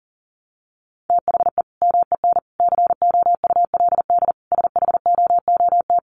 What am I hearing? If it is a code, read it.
Morse code, heard as a keyed tone, about 40 wpm, 700 Hz. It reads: THE MEN COULD SHOOT
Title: Sentences from Top 1200 Words 40wpm